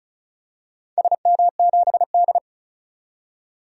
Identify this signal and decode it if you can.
Morse code, heard as a keyed tone, about 34 wpm, 700 Hz. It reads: SM7D
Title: DXCC-100 Call Signs - 4 Characters - 35wpm